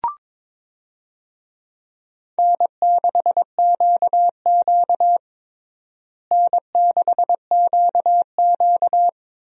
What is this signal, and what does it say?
Morse code, heard as a keyed tone, about 22 wpm, 700 Hz. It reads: N6QQ N6QQ
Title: USA Call Signs - 4 Characters - Repeated 2x - 22wpm